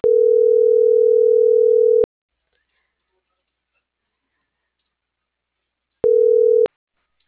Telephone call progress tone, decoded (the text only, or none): ringback tone